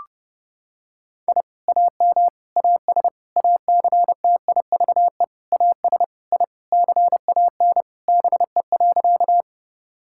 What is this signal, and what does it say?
Morse code, heard as a keyed tone, about 30 wpm, 700 Hz. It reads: I AM AS ACTIVE AS I CAN BE.
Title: Sentences from Top 1000 Words - Review Part 6 - 30wpm